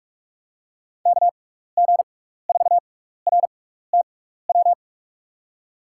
Morse code, heard as a keyed tone, about 46 wpm, spaced as for 15 wpm, 700 Hz. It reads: KG4RTW
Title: USA Call Signs - 6 Characters - 6x Character Spacing - 45wpm